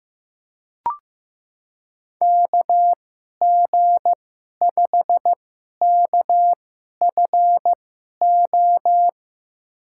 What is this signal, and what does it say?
Morse code, heard as a keyed tone, about 15 wpm, spaced as for 11 wpm, 700 Hz. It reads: KG5KFO
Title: USA Call Signs - 6 Characters - 2x Character Spacing - 15wpm